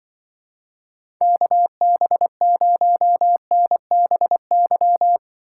Morse code, heard as a keyed tone, about 23 wpm, 700 Hz. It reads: KB0NBY
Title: Speed Racing - USA 2x3 Call Signs 20wpm